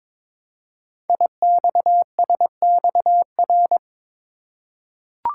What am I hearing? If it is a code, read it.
Morse code, heard as a keyed tone, about 22 wpm, 700 Hz. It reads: IXSXR
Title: Code Groups - 5 Letter:Numbers - 22wpm